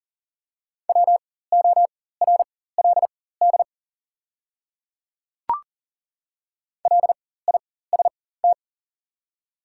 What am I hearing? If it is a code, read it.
Morse code, heard as a keyed tone, about 37 wpm, spaced as for 18 wpm, 690 Hz. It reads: WORLD LIST
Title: 4x Character Spacing - Top 500 Words - 40wpm